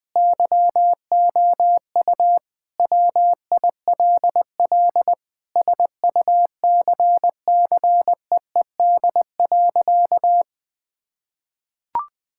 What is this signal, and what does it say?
Morse code, heard as a keyed tone, about 20 wpm, 700 Hz. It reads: YOU WILL SUCCEED.
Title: Sentences with 8-Word Phrases - Spoken Beginning - 20wpm